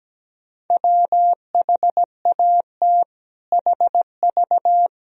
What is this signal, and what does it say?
Morse code, heard as a keyed tone, about 17 wpm, 700 Hz. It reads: WHAT HV
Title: Sentences from Top 500 Words and Abbreviations 17wpm